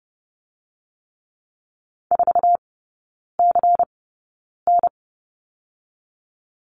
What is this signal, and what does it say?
Morse code, heard as a keyed tone, about 30 wpm, spaced as for 9 wpm, 700 Hz. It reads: 4CN
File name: Lesson 16e - Letter c - Three Characters 30wpm